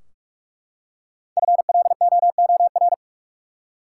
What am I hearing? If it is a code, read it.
Morse code, heard as a keyed tone, about 43 wpm, 700 Hz. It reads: FLOOR
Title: Five Letter Words - 45wpm